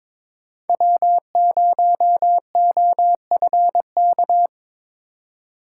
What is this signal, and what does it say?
Morse code, heard as a keyed tone, about 22 wpm, 700 Hz. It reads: W0OFK
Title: USA Call Signs - 5 Characters - 3x Character Spacing - 22wpm